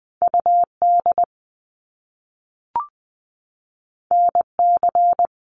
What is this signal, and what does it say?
Morse code, heard as a keyed tone, about 20 wpm, 700 Hz. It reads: UD NC
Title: Top Two Letter Combinations in Words 20wpm